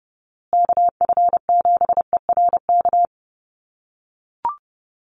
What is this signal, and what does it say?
Morse code, heard as a keyed tone, about 30 wpm, 700 Hz. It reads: KF7ERK
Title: USA Call Signs Repeated 3x 30wpm